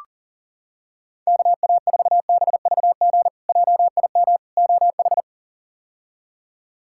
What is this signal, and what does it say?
Morse code, heard as a keyed tone, about 40 wpm, 700 Hz. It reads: KA4BVG JIM OH
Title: K1USN SST QSO Practice - 40wpm